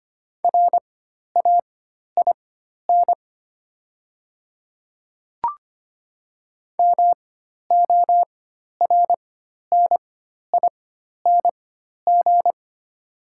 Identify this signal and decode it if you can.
Morse code, heard as a keyed tone, about 25 wpm, spaced as for 12 wpm, 700 Hz. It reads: RAIN MORNING
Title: 4x Character Spacing - Top 500 Words - 25wpm